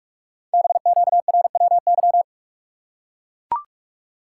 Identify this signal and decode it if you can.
Morse code, heard as a keyed tone, about 44 wpm, 700 Hz. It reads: BQRWY
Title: Code Groups - 5 Letters - 45wpm